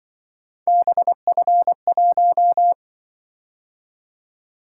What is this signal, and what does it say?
Morse code, heard as a keyed tone, about 24 wpm, 700 Hz. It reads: BF1
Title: Speed-Racing - Three Letter-Number Combinations - 20wpm